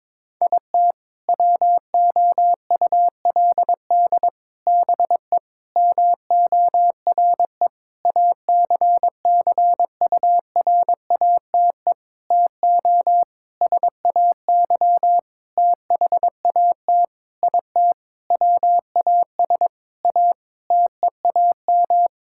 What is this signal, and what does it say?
Morse code, heard as a keyed tone, about 22 wpm, 700 Hz. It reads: IT WOULD BE MORE ACCURATE TO SAY THAT IT WAS A TEAM
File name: practice / 22wpm / Sentences with 8-Word Phrases - 22wpm